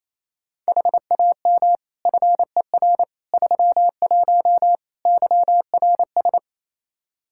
Morse code, heard as a keyed tone, about 28 wpm, 700 Hz. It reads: HAM FER 31 YRS
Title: Practice QSO - Exchange 3 - More Info - 28wpm